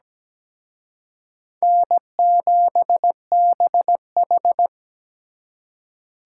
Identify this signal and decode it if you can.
Morse code, heard as a keyed tone, about 20 wpm, 700 Hz. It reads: N7BH
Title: Speed Racing - USA 1x2 Call Signs 17wpm